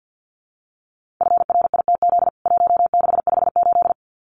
Morse code, heard as a keyed tone, about 50 wpm, 700 Hz. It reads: FRITZ 1658
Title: CWT Exchange 50wpm